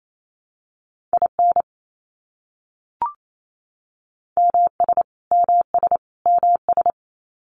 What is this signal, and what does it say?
Morse code, heard as a keyed tone, about 28 wpm, 700 Hz. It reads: IN MS MS MS
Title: US States Repeated 3x 28wpm